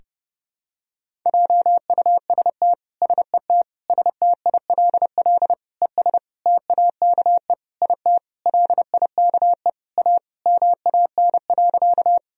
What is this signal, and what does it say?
Morse code, heard as a keyed tone, about 32 wpm, 700 Hz. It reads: JUST SET STILL ES TAKE IT LIKE A MAN.
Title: Sentences from Top 300 Words and Abbreviations 30wpm